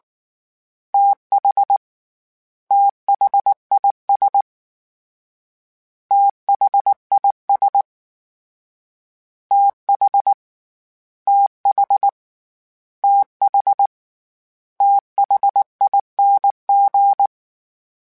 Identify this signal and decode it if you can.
Morse code, heard as a keyed tone, about 19 wpm, 800 Hz. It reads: TH THIS THIS TH TH TH THING
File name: TH / TH mp3s / TH Prefixes 3x2 @18